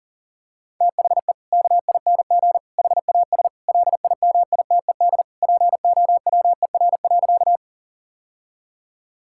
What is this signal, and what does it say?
Morse code, heard as a keyed tone, about 40 wpm, 700 Hz. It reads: THE KING HAS LIMITED POWER.
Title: Sentences from Top 1000 Words - Review Part 6 - 40wpm